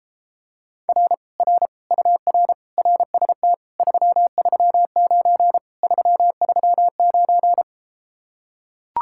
Morse code, heard as a keyed tone, about 34 wpm, 700 Hz. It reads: R R UR RST 339 339